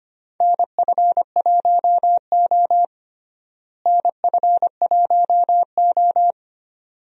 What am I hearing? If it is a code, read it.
Morse code, heard as a keyed tone, about 25 wpm, 700 Hz. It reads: NF1O NF1O
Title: USA Call Signs - 4 Characters - Repeated 2x - 25wpm